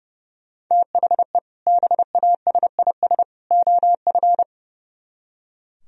Morse code, heard as a keyed tone, about 34 wpm, 700 Hz. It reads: THE BASIS OF
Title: Speed-Racing - Sets of 3 Words 30wpm